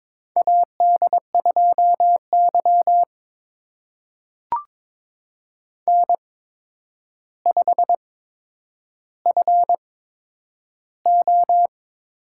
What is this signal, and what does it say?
Morse code, heard as a keyed tone, about 22 wpm, spaced as for 9 wpm, 700 Hz. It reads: AD2Y N5FO
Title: USA Call Signs - 4 Characters - 8x Character Spacing - 22wpm